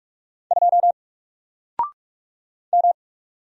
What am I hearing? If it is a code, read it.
Morse code, heard as a keyed tone, about 44 wpm, 700 Hz. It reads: 2 M
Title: Single Character (Letter:Number:Punctuation:Prosign) 45wpm